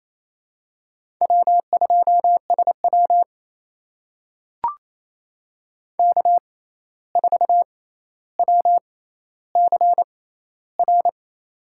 Morse code, heard as a keyed tone, about 28 wpm, spaced as for 13 wpm, 700 Hz. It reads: W2SW K4WCR
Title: USA Call Signs - 6x Character Spacing - 28wpm